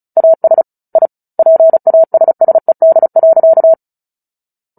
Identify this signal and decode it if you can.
Morse code, heard as a keyed tone, about 34 wpm, 650 Hz. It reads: AS I PASSED.
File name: Book Two / 09-Chapter 9 - 35 WPM